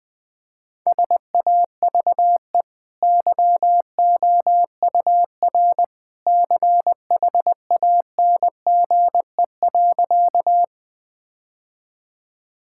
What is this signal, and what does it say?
Morse code, heard as a keyed tone, about 21 wpm, 700 Hz. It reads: SAVE YOUR CHANGE.
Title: Easy Sentences with Hints - 20wpm